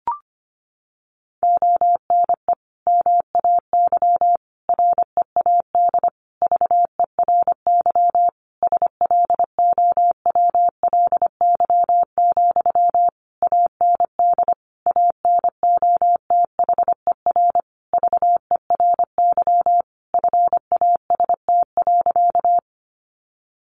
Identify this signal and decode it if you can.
Morse code, heard as a keyed tone, about 25 wpm, 700 Hz. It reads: ONE MAY READ VERY SLOWLY, AND ANOTHER VERY FAST.
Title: Sentences from Top 1100 Words 25wpm